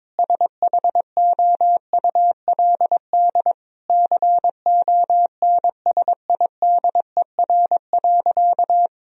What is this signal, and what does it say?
Morse code, heard as a keyed tone, about 21 wpm, 700 Hz. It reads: SHOULD CONSIDER.
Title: Sentences from Top 1000 Words - Review Part 4 - 22wpm